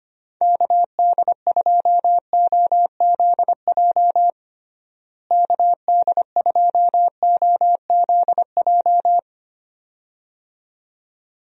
Morse code, heard as a keyed tone, about 25 wpm, 700 Hz. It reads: KD2OZJ KD2OZJ
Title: USA Call Signs - 6 Characters - Repeated 2x - 25wpm